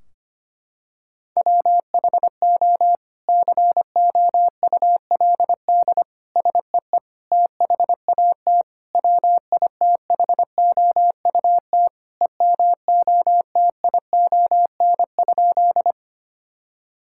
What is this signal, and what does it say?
Morse code, heard as a keyed tone, about 25 wpm, 700 Hz. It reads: WHO COULD SEE THAT WITHOUT EMOTION?
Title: Sentences from Top 1600 Words 25wpm